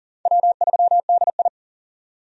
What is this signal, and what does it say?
Morse code, heard as a keyed tone, about 40 wpm, 700 Hz. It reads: W3DI
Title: USA Call Signs - 4 Characters - 40wpm